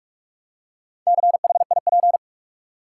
Morse code, heard as a keyed tone, about 46 wpm, 700 Hz. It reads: CHIP
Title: Four Letter Words - 45wpm